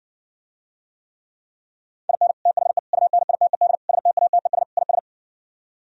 Morse code, heard as a keyed tone, about 69 wpm, 700 Hz. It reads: IN THE UNITED STATES IS